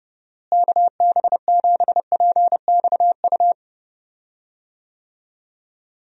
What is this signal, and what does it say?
Morse code, heard as a keyed tone, about 30 wpm, 700 Hz. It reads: KB7PXU
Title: USA Call Signs - 6 Characters - Repeated 3x - 30wpm